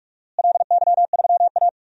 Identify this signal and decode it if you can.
Morse code, heard as a keyed tone, about 46 wpm, 700 Hz. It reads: LY3A
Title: DXCC-100 Call Signs - 4 Characters - Repeated 2x - 45wpm